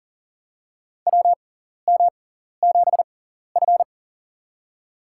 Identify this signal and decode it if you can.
Morse code, heard as a keyed tone, about 40 wpm, spaced as for 14 wpm, 700 Hz. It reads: WM7F
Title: USA Call Signs - 4 Characters - 6x Character Spacing - 40wpm